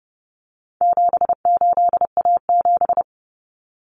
Morse code, heard as a keyed tone, about 30 wpm, 700 Hz. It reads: 78A7
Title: Common Cut Numbers - 4-Digit 30wpm